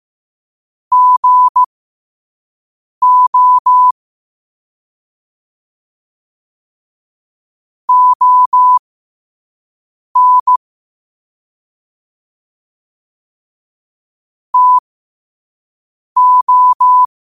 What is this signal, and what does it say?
Morse code, heard as a keyed tone, about 15 wpm, spaced as for 5 wpm, 1000 Hz. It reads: GO ON TO